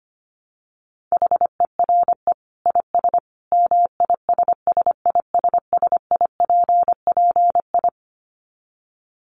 Morse code, heard as a keyed tone, about 25 wpm, 700 Hz. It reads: HERE IS MISSISSIPPI
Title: Dit Challenge 25wpm